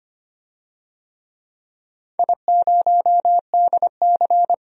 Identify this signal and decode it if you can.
Morse code, heard as a keyed tone, about 25 wpm, 700 Hz. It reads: I0DC
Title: DXCC-100 Call Signs - 4 Characters - 25wpm